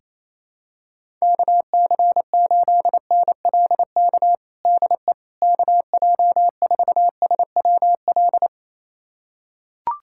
Morse code, heard as a keyed tone, about 28 wpm, 700 Hz. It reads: KC8NLK DE KJ4SWL
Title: USA Callsign DE Callsign 28wpm